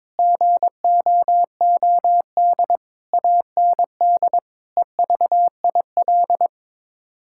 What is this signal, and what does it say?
Morse code, heard as a keyed tone, about 22 wpm, 700 Hz. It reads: GOOD AND EVIL
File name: Sets of 9 Words - 22wpm